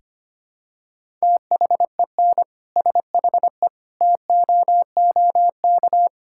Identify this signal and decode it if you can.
Morse code, heard as a keyed tone, about 25 wpm, 700 Hz. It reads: THEN SHE TOOK
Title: Sentences from Top 1600 Words 25wpm